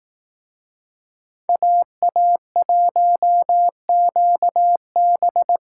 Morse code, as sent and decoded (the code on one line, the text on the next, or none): .- .- .---- --.- -...
AA1QB